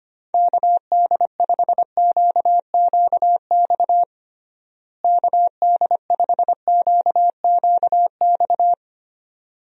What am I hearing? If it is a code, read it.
Morse code, heard as a keyed tone, about 25 wpm, 700 Hz. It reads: KD5QQX KD5QQX
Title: USA Call Signs - 6 Characters - Repeated 2x - 25wpm